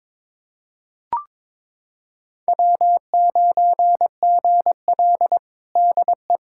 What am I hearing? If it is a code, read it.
Morse code, heard as a keyed tone, about 22 wpm, 700 Hz. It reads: W9GL DE
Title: Practice QSO - Exchange 4 - Ending - 22wpm